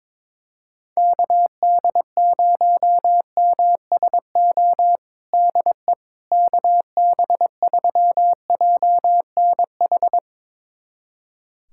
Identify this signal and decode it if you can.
Morse code, heard as a keyed tone, about 22 wpm, 700 Hz. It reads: KD0MSO DE KB3JNH
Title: USA Callsign DE Callsign - 6 Characters - 22wpm